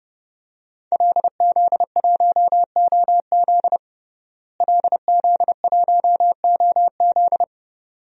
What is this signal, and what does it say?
Morse code, heard as a keyed tone, about 32 wpm, 700 Hz. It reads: LZ1OZ LZ1OZ
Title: DXCC-100 Call Signs - 5 Characters - Repeated 2x - 30wpm